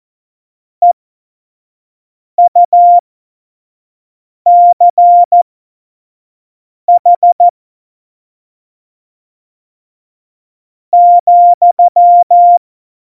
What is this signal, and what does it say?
Morse code, heard as a keyed tone, about 14 wpm, spaced as for 5 wpm, 700 Hz. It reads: EUCH ,
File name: Morsekurs / Witze / Witz 19. Juni 2018, 25 F70